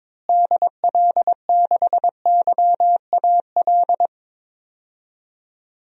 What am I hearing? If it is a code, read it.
Morse code, heard as a keyed tone, about 22 wpm, 700 Hz. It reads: DL6YAL
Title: DXCC-100 Call Signs - 6 Characters - Repeated 3x - 22wpm